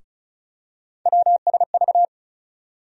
Morse code, heard as a keyed tone, about 34 wpm, 700 Hz. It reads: WSV
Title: USA Call Sign Postfixes - 3 Letters 35wpm